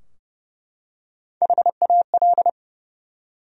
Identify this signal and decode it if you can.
Morse code, heard as a keyed tone, about 30 wpm, 700 Hz. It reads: HAL
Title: USA Common Names - Max 5 Letters - 30wpm